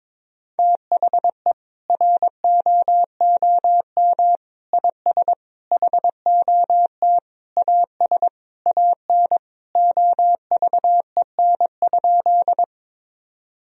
Morse code, heard as a keyed tone, about 22 wpm, 700 Hz. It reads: THE ROOM IS HOT AS AN OVEN?
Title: Sentences from Top 1000 Words - Review Part 4 - 22wpm